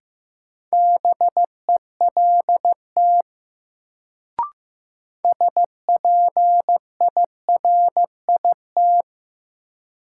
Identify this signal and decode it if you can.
Morse code, heard as a keyed tone, about 15 wpm, 700 Hz. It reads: BELT SPIRIT
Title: Sentences from Top 1400 Words - New Word Drill Alt - 15wpm